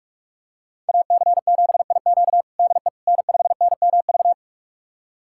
Morse code, heard as a keyed tone, about 46 wpm, 700 Hz. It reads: AC7IQ DE N5NMV
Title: USA Callsign DE Callsign - 5 Characters - 45wpm